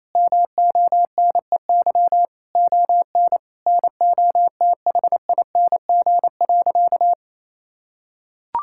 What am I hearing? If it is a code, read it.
Morse code, heard as a keyed tone, about 28 wpm, 700 Hz. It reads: MONEY ON NOTHING.
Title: Sentences from Top 1000 Words - Review Part 5 - 28wpm